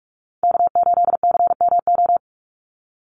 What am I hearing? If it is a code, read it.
Morse code, heard as a keyed tone, about 44 wpm, 700 Hz. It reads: K8CMO
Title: Call Signs Repeated 3x - 45wpm